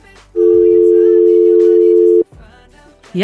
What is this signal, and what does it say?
Dial tone (North America), 350 + 440 Hz.